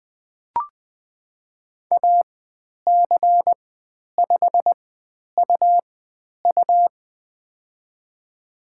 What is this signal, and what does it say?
Morse code, heard as a keyed tone, about 20 wpm, spaced as for 10 wpm, 700 Hz. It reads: AC5UU
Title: 109 Lesson 18g - Letter m - Call Signs 20wpm